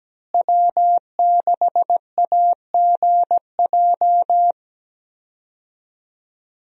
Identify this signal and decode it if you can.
Morse code, heard as a keyed tone, about 17 wpm, 700 Hz. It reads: W6AGJ
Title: USA Call Signs - 5 Characters - Repeated 3x - 17wpm